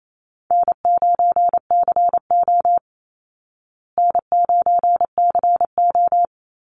Morse code, heard as a keyed tone, about 28 wpm, 700 Hz. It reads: N9CO N9CO
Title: Easy Call Signs Repeated 2x - 28wpm